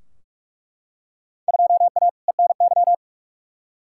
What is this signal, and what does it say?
Morse code, heard as a keyed tone, about 46 wpm, 700 Hz. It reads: 2A ENY
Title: Field Day Exchanges Encore - 45wpm